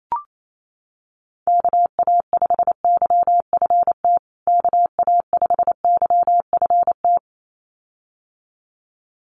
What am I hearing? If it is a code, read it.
Morse code, heard as a keyed tone, about 28 wpm, 700 Hz. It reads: KA5YFT KA5YFT
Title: USA Call Signs Repeated 2x 28wpm